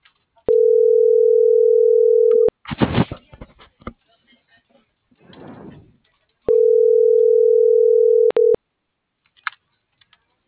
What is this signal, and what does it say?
Ringback tone (North America), 440 + 480 Hz.
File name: CENTRAL MULTIMODAL / out francisco.cobos